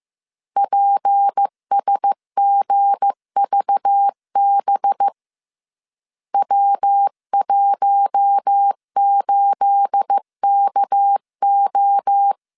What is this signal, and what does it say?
Morse code, heard as a keyed tone, about 15 wpm, 800 Hz. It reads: PSGVB W18KO
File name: CW / DET / 1 9 1500C S C 14wpm 3245